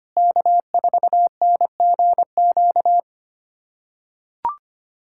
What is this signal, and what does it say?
Morse code, heard as a keyed tone, about 25 wpm, 700 Hz. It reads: K4NGQ
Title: USA Callsign DE Callsign - 5 Characters - 25wpm